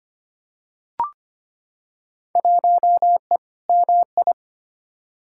Morse code, heard as a keyed tone, about 25 wpm, 700 Hz. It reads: 1E MI
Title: Field Day Exchanges - 25wpm